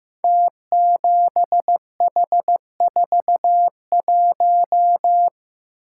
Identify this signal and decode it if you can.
Morse code, heard as a keyed tone, about 15 wpm, 700 Hz. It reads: T7H41
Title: Code Groups - 5 Letter:Numbers - 15wpm